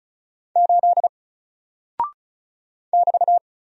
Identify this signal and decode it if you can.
Morse code, heard as a keyed tone, about 36 wpm, 700 Hz. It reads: 8 <BT>
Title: Single Character (Letter:Number:Punctuation:Prosign) Alt 35wpm